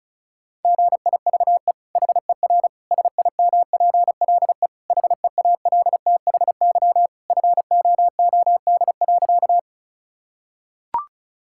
Morse code, heard as a keyed tone, about 36 wpm, 700 Hz. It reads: GIVE HER SIMPLE HEALTHY FOOD.
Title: Sentences from Top 1000 Words - Review Part 5 - 35wpm